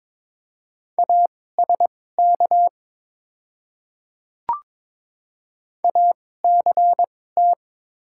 Morse code, heard as a keyed tone, about 22 wpm, spaced as for 16 wpm, 700 Hz. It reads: ASK ACT
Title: 2x Character Spacing - Top 200 Words - 22wpm